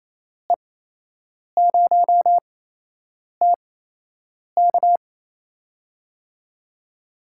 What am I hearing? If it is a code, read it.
Morse code, heard as a keyed tone, about 28 wpm, spaced as for 8 wpm, 700 Hz. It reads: E0TK
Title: USA Call Signs - 5 Characters - 8x Character Spacing - 28wpm